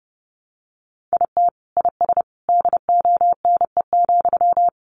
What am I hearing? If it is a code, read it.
Morse code, heard as a keyed tone, about 30 wpm, 700 Hz. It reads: IT IS DONE,
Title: Sentences from Top 600 Words 30wpm